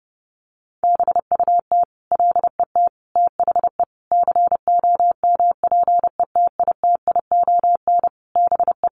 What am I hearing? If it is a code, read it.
Morse code, heard as a keyed tone, about 30 wpm, 700 Hz. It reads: BUT LET THE COMPETITION BE
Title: Sentences from Top 1300 Words 30wpm